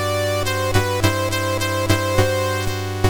There is no code.